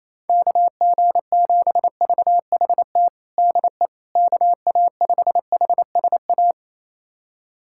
Morse code, heard as a keyed tone, about 28 wpm, 700 Hz. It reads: KG7VHT DE KA5HSA
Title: USA Callsign DE Callsign - 6 Characters - 28wpm